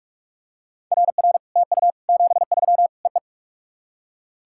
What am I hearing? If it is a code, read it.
Morse code, heard as a keyed tone, about 46 wpm, 700 Hz. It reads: RR TU 73 EE
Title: SOTA QSO Practice 45wpm